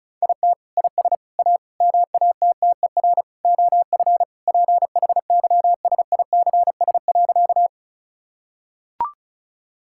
Morse code, heard as a keyed tone, about 36 wpm, 700 Hz. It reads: IT IS A MATTER OF PHYSICS.
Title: Sentences from Top 1000 Words - Review Part 5 - 35wpm